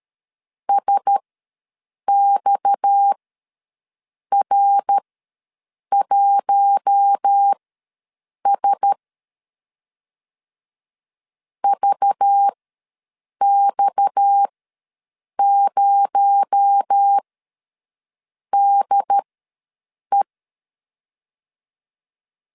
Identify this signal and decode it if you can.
Morse code, heard as a keyed tone, about 13 wpm, spaced as for 6 wpm, 800 Hz. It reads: SXR1S VX0DE
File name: LEZIONI / 6 PKX0 / 6 PKX0 ws 12wpm 3448